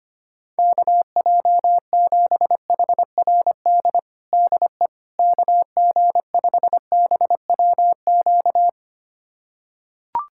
Morse code, heard as a keyed tone, about 25 wpm, 700 Hz. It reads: KJ7HRD DE KG5BWQ
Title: USA Callsign DE Callsign 25wpm